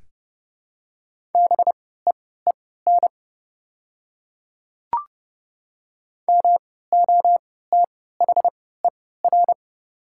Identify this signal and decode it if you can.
Morse code, heard as a keyed tone, about 32 wpm, spaced as for 17 wpm, 700 Hz. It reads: BEEN MOTHER
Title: 3x Character Spacing - Top 200 Words - 30wpm